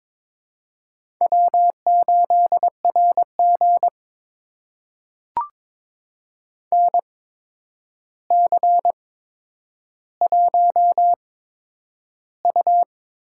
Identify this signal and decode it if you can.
Morse code, heard as a keyed tone, about 22 wpm, spaced as for 9 wpm, 700 Hz. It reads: W8RG NC1U
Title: USA Call Signs - 4 Characters - 8x Character Spacing - 22wpm